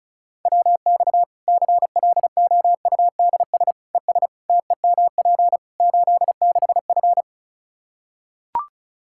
Morse code, heard as a keyed tone, about 36 wpm, 700 Hz. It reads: WX CLOUDS ES TEMP 86F
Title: Practice QSO - Exchange 2 - Info - 35wpm